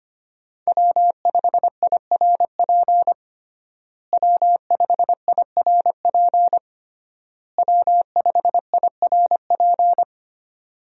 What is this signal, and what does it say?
Morse code, heard as a keyed tone, about 25 wpm, 700 Hz. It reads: W5IRP W5IRP W5IRP
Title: USA Call Signs - 5 Characters - Repeated 3x - 25wpm